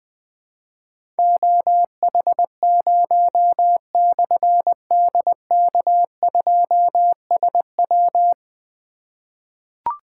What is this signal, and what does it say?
Morse code, heard as a keyed tone, about 20 wpm, 700 Hz. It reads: OH0/DK2SW
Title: Difficult Call Signs 20wpm